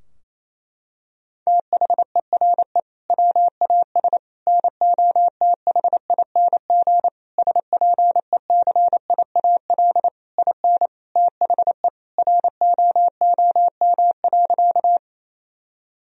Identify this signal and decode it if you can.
Morse code, heard as a keyed tone, about 28 wpm, 700 Hz. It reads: THERE WAS NOTHING SPECIAL IN THE ROOM.
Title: Sentences from Top 1000 Words - Review Part 3 - 28wpm